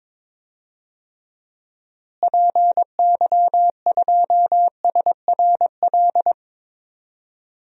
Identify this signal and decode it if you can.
Morse code, heard as a keyed tone, about 22 wpm, 700 Hz. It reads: PY2SRL
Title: DXCC-100 Call Signs - 6 Characters - Repeated 2x - 22wpm